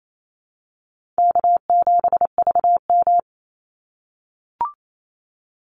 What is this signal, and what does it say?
Morse code, heard as a keyed tone, about 28 wpm, 700 Hz. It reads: K7VM
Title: Call Signs - 28wpm